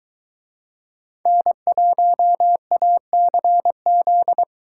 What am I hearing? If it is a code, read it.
Morse code, heard as a keyed tone, about 22 wpm, 700 Hz. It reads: N1ACZ